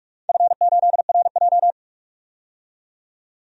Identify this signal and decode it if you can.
Morse code, heard as a keyed tone, about 42 wpm, 700 Hz. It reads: F8RJ
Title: DXCC-100 Call Signs - 4 Characters - 45wpm